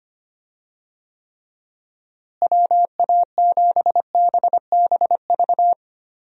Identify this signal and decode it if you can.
Morse code, heard as a keyed tone, about 25 wpm, 700 Hz. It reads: WA7BBV
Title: USA Call Signs - 6 Characters - Repeated 3x - 25wpm